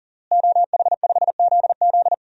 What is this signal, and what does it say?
Morse code, heard as a keyed tone, about 37 wpm, 700 Hz. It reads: OH5ZZ
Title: DXCC-100 Call Signs - 5 Characters - Repeated 3x - 40wpm